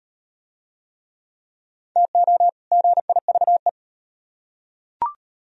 Morse code, heard as a keyed tone, about 38 wpm, 700 Hz. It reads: TO GIVE